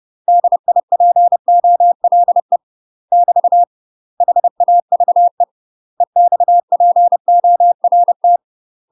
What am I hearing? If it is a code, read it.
Morse code, heard as a keyed tone, about 30 wpm, 700 Hz. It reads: DIPOLE <BT> HAVE EXPORT
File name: UK MASTER QSOs TRAINING / 30WPM / UK MASTER QSOs 0023